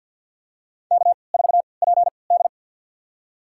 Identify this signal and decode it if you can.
Morse code, heard as a keyed tone, about 50 wpm, spaced as for 28 wpm, 700 Hz. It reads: K4PD
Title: USA Call Signs - 4 Characters - 3x Character Spacing - 50wpm